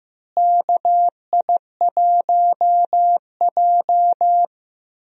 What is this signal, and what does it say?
Morse code, heard as a keyed tone, about 15 wpm, 700 Hz. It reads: KI1J
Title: USA Call Signs - 4 Characters - Repeated 2x - 15wpm